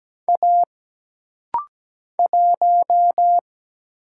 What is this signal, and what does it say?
Morse code, heard as a keyed tone, about 17 wpm, 700 Hz. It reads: A 1
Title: Single Letter-Number Alt - Rapid-Fire 17wpm